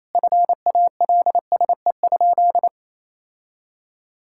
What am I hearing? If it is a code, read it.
Morse code, heard as a keyed tone, about 28 wpm, 700 Hz. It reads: FALSE?
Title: Sentences from Top 1600 Words 28wpm